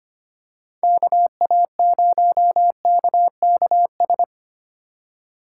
Morse code, heard as a keyed tone, about 25 wpm, 700 Hz. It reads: KA0KKS
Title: USA Call Signs - 4x Character Spacing - 25wpm